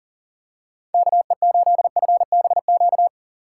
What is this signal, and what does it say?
Morse code, heard as a keyed tone, about 40 wpm, 700 Hz. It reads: KE8FBQ
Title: USA Call Signs - 6 Characters - 4x Character Spacing - 40wpm